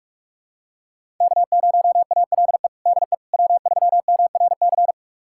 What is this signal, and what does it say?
Morse code, heard as a keyed tone, about 46 wpm, 700 Hz. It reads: K0ALE DE W3MRC
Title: USA Callsign DE Callsign - 5 Characters - 45wpm